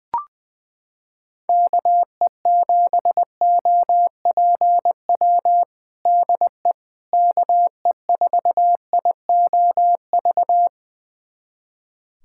Morse code, heard as a keyed tone, about 20 wpm, 700 Hz. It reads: KE7OPW DE KE4IOV
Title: USA Callsign DE Callsign - 6 Characters - 20wpm